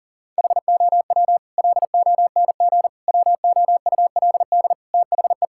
Morse code, heard as a keyed tone, about 40 wpm, 700 Hz. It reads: HOW LONG WOULD THE
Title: Sentences from Top 1000 Words - Review Part 5 - 40wpm